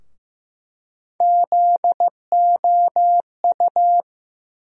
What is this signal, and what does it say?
Morse code, heard as a keyed tone, about 15 wpm, 700 Hz. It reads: ZOU
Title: USA Call Sign Postfixes - 3 Letters 15wpm